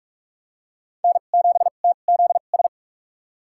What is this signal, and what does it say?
Morse code, heard as a keyed tone, about 46 wpm, spaced as for 32 wpm, 700 Hz. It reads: N7TZS
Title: USA Call Signs - 5 Characters - 2x Character Spacing - 45wpm